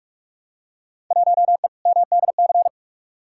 Morse code, heard as a keyed tone, about 44 wpm, 700 Hz. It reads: 1E MDC
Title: Field Day Exchanges Encore - 45wpm